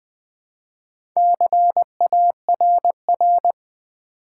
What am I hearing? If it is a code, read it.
Morse code, heard as a keyed tone, about 20 wpm, 700 Hz. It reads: CARR
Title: Build A Word - 2 Letters At A Time - 6-Letter Words - 20wpm